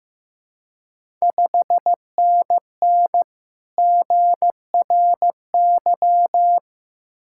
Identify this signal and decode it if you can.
Morse code, heard as a keyed tone, about 15 wpm, 690 Hz. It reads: 5NN GRY
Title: Ontario QSO Party Exchange 15wpm